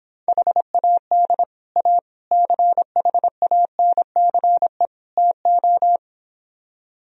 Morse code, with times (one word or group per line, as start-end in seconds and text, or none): 0.3-1.4 HAD
1.8-2.0 A
2.3-4.9 CHANCE
5.2-6.0 TO